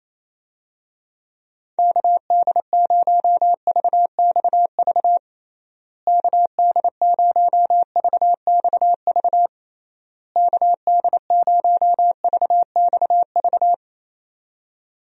Morse code, heard as a keyed tone, about 28 wpm, 700 Hz. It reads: KD0VXV KD0VXV KD0VXV
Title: USA Call Signs - 6 Characters - Repeated 3x - 28wpm